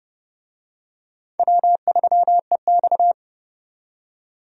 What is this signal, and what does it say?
Morse code, heard as a keyed tone, about 31 wpm, 700 Hz. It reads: W3EX
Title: USA Call Signs - 4 Characters - 30wpm